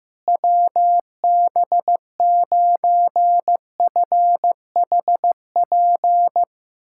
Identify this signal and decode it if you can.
Morse code, heard as a keyed tone, about 15 wpm, 700 Hz. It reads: WB9FHP
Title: USA Callsign DE Callsign - 6 Characters - 15wpm